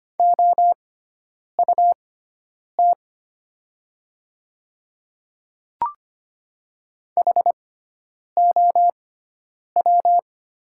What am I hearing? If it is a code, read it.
Morse code, heard as a keyed tone, about 25 wpm, spaced as for 9 wpm, 700 Hz. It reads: OUT HOW
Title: 6x Character Spacing - Top 100 Words - 25wpm